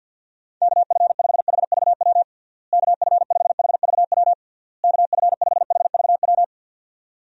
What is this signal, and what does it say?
Morse code, heard as a keyed tone, about 50 wpm, 700 Hz. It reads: KF5HVW KF5HVW KF5HVW
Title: USA Call Signs - 6 Characters - Repeated 3x - 50wpm